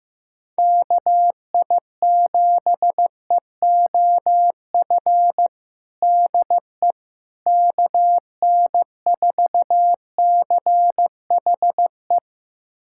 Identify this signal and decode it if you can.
Morse code, heard as a keyed tone, about 15 wpm, 700 Hz. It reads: KI7EOF DE KN4CHE
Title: USA Callsign DE Callsign - 6 Characters - 15wpm